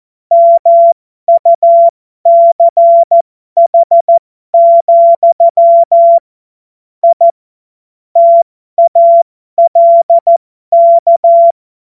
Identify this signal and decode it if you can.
Morse code, heard as a keyed tone, about 14 wpm, spaced as for 12 wpm, 680 Hz. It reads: MUCH, I TALK
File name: Morsekurs / Witze